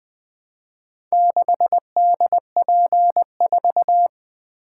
Morse code, heard as a keyed tone, about 20 wpm, 700 Hz. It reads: 6DP4
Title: Four Letter-Number Combinations - 20wpm